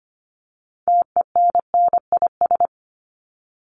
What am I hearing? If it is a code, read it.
Morse code, heard as a keyed tone, about 25 wpm, 700 Hz. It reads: TENNIS
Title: Top 1501-1750 Words 25wpm